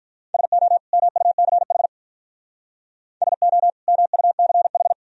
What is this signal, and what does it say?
Morse code, heard as a keyed tone, about 45 wpm, 700 Hz. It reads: SO MUCH SO MUCH